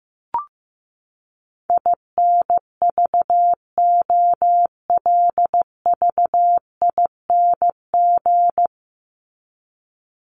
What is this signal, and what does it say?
Morse code, heard as a keyed tone, about 15 wpm, 700 Hz. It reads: INVOLVING
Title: ING Suffix - Top 100 Words with Base Words 15wpm